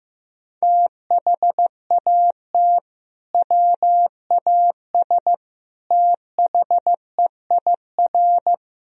Morse code, heard as a keyed tone, about 15 wpm, 700 Hz. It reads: THAT WAS THEIR
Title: Sentences from Top 1000 Words - Review Part 2 - 15wpm